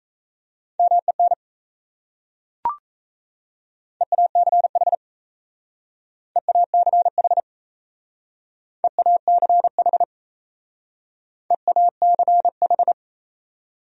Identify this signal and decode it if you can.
Morse code, heard as a keyed tone, about 34 wpm, 700 Hz. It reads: MEN EACH EACH EACH EACH
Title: Speed-Racing Top 500 Words 28wpm